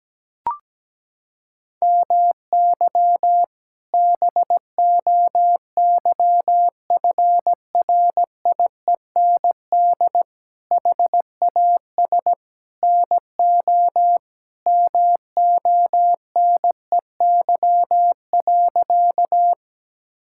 Morse code, heard as a keyed tone, about 17 wpm, 700 Hz. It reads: MY BOYFRIEND HAS NO MONEY.
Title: Sentences from Top 1000 Words - Review Part 5 - 17wpm